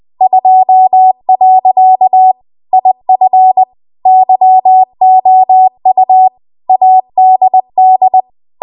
Morse code, heard as a keyed tone, about 20 wpm, 750 Hz. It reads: 2. IF YOU ADD
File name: Archive / 20 WPM / 20WPM